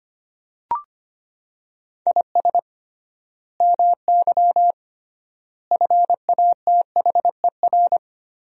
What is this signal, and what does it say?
Morse code, heard as a keyed tone, about 25 wpm, 700 Hz. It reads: IS MY FATHER
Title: 3x Word-Spacing - Sentences from Top 300 Words - 25wpm